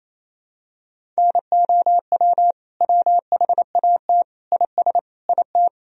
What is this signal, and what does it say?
Morse code, heard as a keyed tone, about 28 wpm, 700 Hz. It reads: NOW WHAT IS IT
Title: Sentences from Top 200 Words and Abbreviations 28wpm